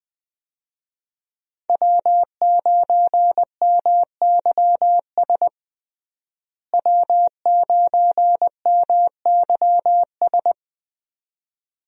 Morse code, heard as a keyed tone, about 20 wpm, 700 Hz. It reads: W9MYS W9MYS
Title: USA Call Signs - 5 Characters - Repeated 2x - 20wpm